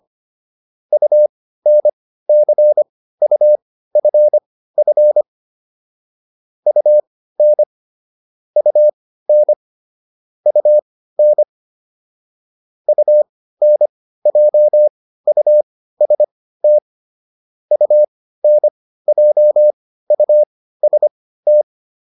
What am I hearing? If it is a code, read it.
Morse code, heard as a keyed tone, about 25 wpm, spaced as for 15 wpm, 600 Hz. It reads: UNCUFF UN UN UN UNJUST UNJUST